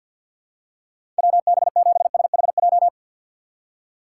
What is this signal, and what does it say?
Morse code, heard as a keyed tone, about 47 wpm, 700 Hz. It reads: WB7SHJ
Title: USA Call Signs - 6 Characters - 6x Character Spacing - 50wpm